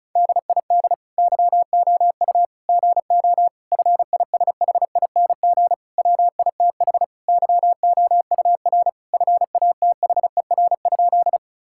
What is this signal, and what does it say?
Morse code, heard as a keyed tone, about 36 wpm, 700 Hz. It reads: DID YOU GO FISHING WITH YOUR FATHER?
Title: Sentences from Top 1000 Words - Review Part 5 - 35wpm